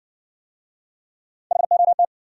Morse code, heard as a keyed tone, about 59 wpm, 700 Hz. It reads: HOT